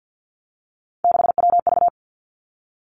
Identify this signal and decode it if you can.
Morse code, heard as a keyed tone, about 50 wpm, 700 Hz. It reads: 6WV
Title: Three Challenging Letter-Number Combinations 50wpm